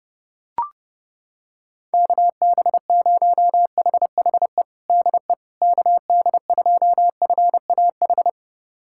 Morse code, heard as a keyed tone, about 32 wpm, 700 Hz. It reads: KB0HHE DE KD2FAH
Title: USA Callsign DE Callsign 30wpm